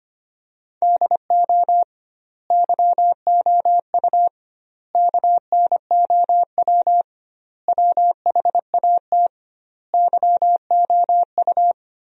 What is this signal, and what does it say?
Morse code, heard as a keyed tone, about 25 wpm, 700 Hz. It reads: DO YOU KNOW WHAT YOU
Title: 2x Word-Spacing - Sentences from Top 100 Words - 25wpm